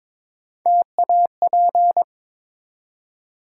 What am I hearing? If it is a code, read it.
Morse code, heard as a keyed tone, about 23 wpm, 700 Hz. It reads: TAP
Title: Three Letter Words - 22wpm